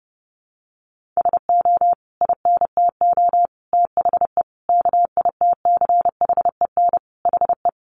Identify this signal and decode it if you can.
Morse code, heard as a keyed tone, about 30 wpm, 700 Hz. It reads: SO INTO THE KITCHEN HE
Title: Sentences from Top 1400 Words 30wpm